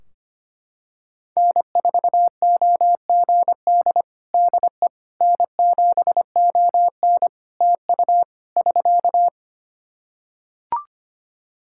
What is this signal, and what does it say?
Morse code, heard as a keyed tone, about 25 wpm, 700 Hz. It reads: N4OGD DE N7ON TU <SK>
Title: Practice QSO - Exchange 4 - Ending - 25wpm